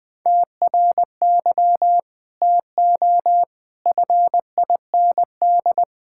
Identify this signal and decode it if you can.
Morse code, heard as a keyed tone, about 20 wpm, 700 Hz. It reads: TRY TO FIND
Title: Sentences with 3-Word Phrases - 20wpm